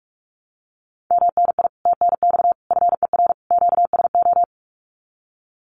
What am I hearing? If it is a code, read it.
Morse code, heard as a keyed tone, about 46 wpm, 700 Hz. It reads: MNI TNX FER QSO
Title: Common QSO Phrases Encore 45wpm